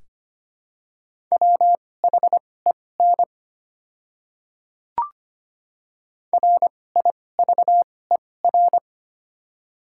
Morse code, heard as a keyed tone, about 25 wpm, spaced as for 18 wpm, 700 Hz. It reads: WHEN RIVER
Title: 2x Character Spacing - Top 500 Words - 25wpm